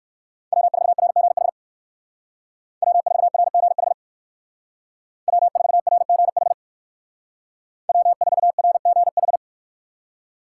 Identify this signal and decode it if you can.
Morse code, heard as a keyed tone, about 57 wpm, 700 Hz. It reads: W4RGH W4RGH W4RGH W4RGH